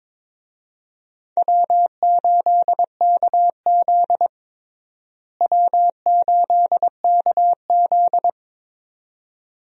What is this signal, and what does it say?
Morse code, heard as a keyed tone, about 22 wpm, 700 Hz. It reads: W8KZ W8KZ
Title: USA Call Signs - 4 Characters - Repeated 2x - 22wpm